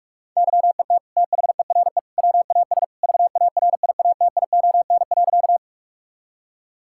Morse code, heard as a keyed tone, about 46 wpm, 700 Hz. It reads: YET THERE WAS VARIATION.
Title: Sentences from Top 1600 Words 45wpm